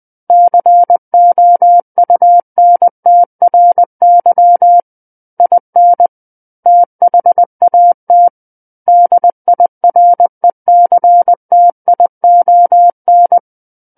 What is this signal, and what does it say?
Morse code, heard as a keyed tone, about 20 wpm, 700 Hz. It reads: COUNTRY IN THAT DIRECTION
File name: 20 WPM